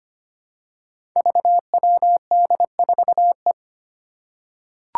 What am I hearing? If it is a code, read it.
Morse code, heard as a keyed tone, about 25 wpm, 700 Hz. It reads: VWD4E
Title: Five Challenging Letter-Number Combinations 25wpm